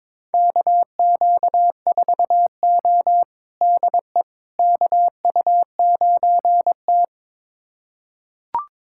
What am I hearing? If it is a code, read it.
Morse code, heard as a keyed tone, about 22 wpm, 700 Hz. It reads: KQ4O DE KU9T
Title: USA Callsign DE Callsign - 4 Characters - 22wpm